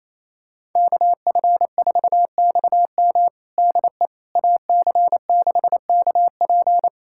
Morse code, heard as a keyed tone, about 28 wpm, 700 Hz. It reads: KF4XM DE AC6KP
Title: USA Callsign DE Callsign - 5 Characters - 28wpm